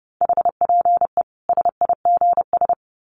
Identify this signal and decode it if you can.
Morse code, heard as a keyed tone, about 30 wpm, 700 Hz. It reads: HPE SIGS
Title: Common QSO Phrases 30wpm